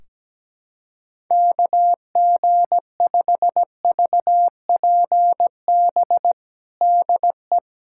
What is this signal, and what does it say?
Morse code, heard as a keyed tone, about 17 wpm, 700 Hz. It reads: KG5VPB DE
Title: USA Callsign DE Callsign 17wpm